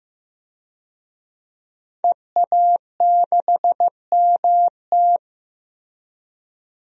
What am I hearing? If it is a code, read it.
Morse code, heard as a keyed tone, about 15 wpm, 700 Hz. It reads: EA6MT
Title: DXCC-100 Call Signs - 5 Characters - Repeated 2x - 15wpm